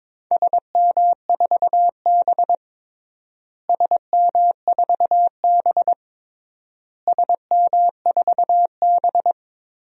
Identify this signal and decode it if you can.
Morse code, heard as a keyed tone, about 22 wpm, 700 Hz. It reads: SM4B SM4B SM4B
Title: DXCC-100 Call Signs - 4 Characters - Repeated 3x - 22wpm